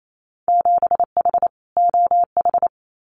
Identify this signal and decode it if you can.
Morse code, heard as a keyed tone, about 28 wpm, 700 Hz. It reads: 7H OH
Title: Winter Field Day Exchanges Encore 28wpm